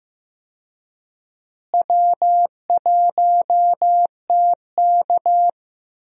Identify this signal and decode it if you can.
Morse code, heard as a keyed tone, about 15 wpm, 700 Hz. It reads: W1TK
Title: USA Call Signs - 4 Characters - Repeated 2x - 15wpm